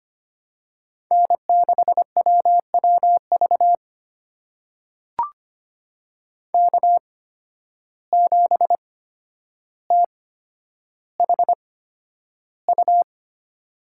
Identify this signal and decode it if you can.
Morse code, heard as a keyed tone, about 25 wpm, spaced as for 11 wpm, 700 Hz. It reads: N6WWV K7THU
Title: USA Call Signs - 5 Characters - 8x Character Spacing - 25wpm